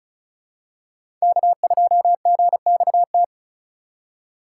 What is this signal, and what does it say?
Morse code, heard as a keyed tone, about 36 wpm, 700 Hz. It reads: K2GXT
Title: USA Call Signs - 5 Characters - 2x Character Spacing - 35wpm